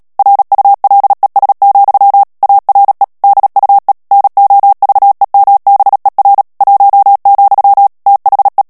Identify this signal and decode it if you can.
Morse code, heard as a keyed tone, about 37 wpm, 800 Hz. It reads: RULES, ARE DUE NOVEMBER 1, THE
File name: morse mp3 / ANGLAIS texte clair 3 vitesse 35 mots minute